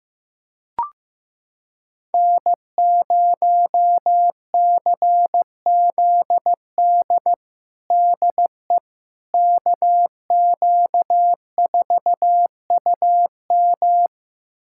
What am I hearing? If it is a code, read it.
Morse code, heard as a keyed tone, about 15 wpm, 700 Hz. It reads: N0CZD DE KQ4UM
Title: USA Callsign DE Callsign - 5 Characters - 15wpm